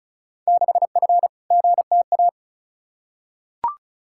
Morse code, heard as a keyed tone, about 34 wpm, 700 Hz. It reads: 6F GTA
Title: Field Day Exchanges Encore - 35wpm